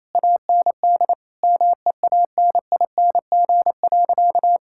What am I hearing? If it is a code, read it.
Morse code, heard as a keyed tone, about 28 wpm, 700 Hz. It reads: AND MEANING.
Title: Sentences from Top 1000 Words - Review Part 6 - 28wpm